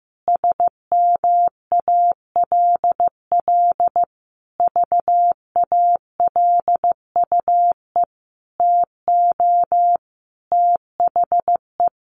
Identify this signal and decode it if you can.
Morse code, heard as a keyed tone, about 15 wpm, 700 Hz. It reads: SMALL VALUE TO THE
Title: Sentences from Top 1500 Words 15wpm